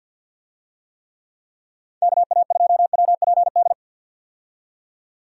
Morse code, heard as a keyed tone, about 52 wpm, 700 Hz. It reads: KA2WPD
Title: USA Call Signs - 6 Characters - 50wpm